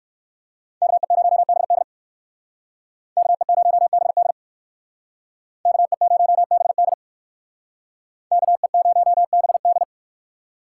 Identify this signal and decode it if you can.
Morse code, heard as a keyed tone, about 58 wpm, 700 Hz. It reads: KE0BD KE0BD KE0BD KE0BD